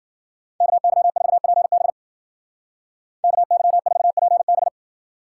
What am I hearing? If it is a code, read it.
Morse code, heard as a keyed tone, about 51 wpm, 700 Hz. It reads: KY4PB KY4PB